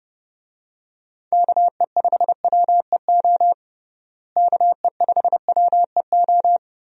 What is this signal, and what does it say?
Morse code, heard as a keyed tone, about 30 wpm, 700 Hz. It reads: KE5WEO KE5WEO
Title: USA Call Signs - 6 Characters - Repeated 2x - 30wpm